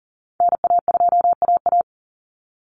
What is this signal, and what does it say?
Morse code, heard as a keyed tone, about 37 wpm, 700 Hz. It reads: NA2AA
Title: Call Signs Repeated 2x - 40wpm